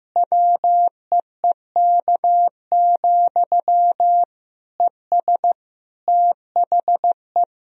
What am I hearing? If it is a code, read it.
Morse code, heard as a keyed tone, about 16 wpm, 700 Hz. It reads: WEEK, ES THE
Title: Sentences from Top 500 Words and Abbreviations 15wpm